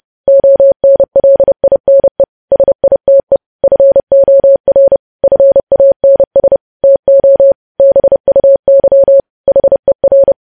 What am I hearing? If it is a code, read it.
Morse code, heard as a keyed tone, about 30 wpm, 550 Hz. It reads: ONLINE SITE FOR FANS TO BUY HER
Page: news headlines in morse code at 30 wpm